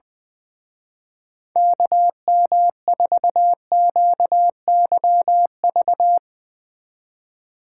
Morse code, heard as a keyed tone, about 23 wpm, 700 Hz. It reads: KM4QYV
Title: Speed Racing - USA 2x3 Call Signs 20wpm